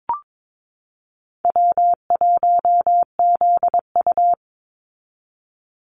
Morse code, heard as a keyed tone, about 22 wpm, 700 Hz. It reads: W1ZU
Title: NA SOTA Activator Callsigns 22wpm